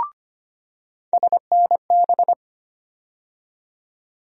Morse code, heard as a keyed tone, about 25 wpm, 700 Hz. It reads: SNB
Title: USA Call Sign Postfixes - 3 Letters 25wpm